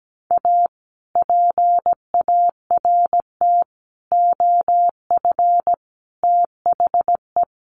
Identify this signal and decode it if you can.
Morse code, heard as a keyed tone, about 17 wpm, 700 Hz. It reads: A PART OF THE
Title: Sets of 4 Words 17wpm